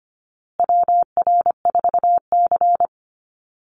Morse code, heard as a keyed tone, about 25 wpm, 700 Hz. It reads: WR4C
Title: USA Call Signs Repeated 3x 25wpm